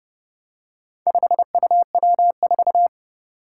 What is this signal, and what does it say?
Morse code, heard as a keyed tone, about 30 wpm, 700 Hz. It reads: 5UW4
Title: Four Challenging Letter-Number Combinations 30wpm